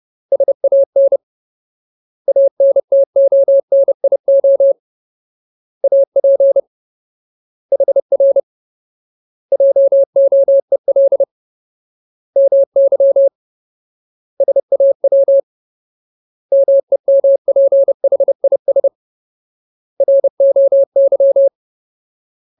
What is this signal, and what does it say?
Morse code, heard as a keyed tone, about 30 wpm, 550 Hz. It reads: SAN ANTONIO AP HR JOEL MY SAW MEMPHIS ROY
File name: Ultimate Word List / 30WPM Ultimate